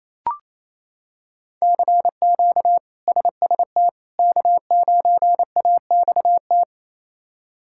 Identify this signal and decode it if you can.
Morse code, heard as a keyed tone, about 28 wpm, 700 Hz. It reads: CQ SST K9AXT
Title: K1USN SST QSO Practice - 28wpm